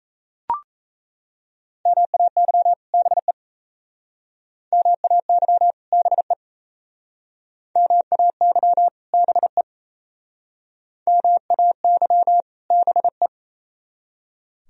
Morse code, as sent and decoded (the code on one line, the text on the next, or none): -- .- -.-- / -... / -- .- -.-- / -... / -- .- -.-- / -... / -- .- -.-- / -... .
MAY BE MAY BE MAY BE MAY BE